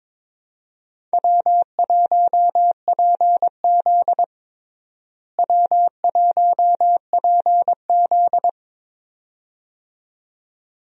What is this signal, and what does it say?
Morse code, heard as a keyed tone, about 22 wpm, 700 Hz. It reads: W1PZ W1PZ
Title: USA Call Signs - 4 Characters - Repeated 2x - 22wpm